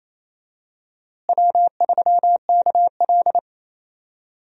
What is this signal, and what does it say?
Morse code, heard as a keyed tone, about 27 wpm, 700 Hz. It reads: W3KL
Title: USA Call Signs - 4 Characters - Repeated 3x - 28wpm